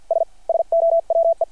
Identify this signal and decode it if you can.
Morse code, heard as a keyed tone, about 52 wpm, spaced as for 44 wpm, 660 Hz. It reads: S SOWE
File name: archived / CRSnet